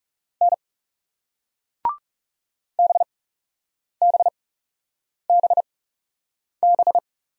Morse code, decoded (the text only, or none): N BBBB